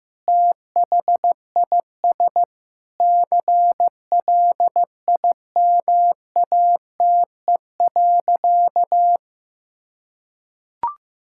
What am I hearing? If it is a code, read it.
Morse code, heard as a keyed tone, about 15 wpm, 700 Hz. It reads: THIS CLIMATE.
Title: Sentences from Top 1600 Words 15wpm